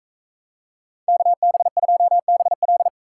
Morse code, heard as a keyed tone, about 41 wpm, 700 Hz. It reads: KB2BL